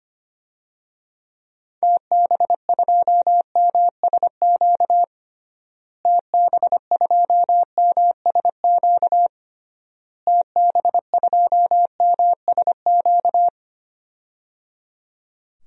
Morse code, heard as a keyed tone, about 25 wpm, 700 Hz. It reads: TB2MSQ TB2MSQ TB2MSQ
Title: DXCC-100 Call Signs - 6 Characters - Repeated 3x - 25wpm